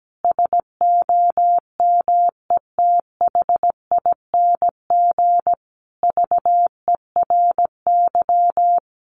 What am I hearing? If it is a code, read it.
Morse code, heard as a keyed tone, about 17 wpm, 700 Hz. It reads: SOMETHING VERY
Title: Sentences from Top 1400 Words 17wpm